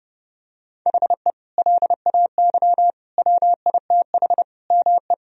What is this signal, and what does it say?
Morse code, heard as a keyed tone, about 32 wpm, 700 Hz. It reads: HE LAY WITH ME
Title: Sentences from Top 1000 Words - Review Part 2 - 30wpm